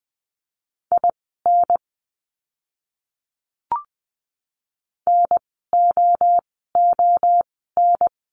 Morse code, heard as a keyed tone, about 20 wpm, spaced as for 15 wpm, 700 Hz. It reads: IN NOON
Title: 030 Lesson 6f - Letter i - Words 20wpm